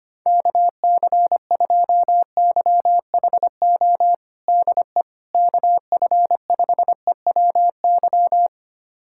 Morse code, heard as a keyed tone, about 25 wpm, 700 Hz. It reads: KC2YHO DE KF5EWY
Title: USA Callsign DE Callsign - 6 Characters - 25wpm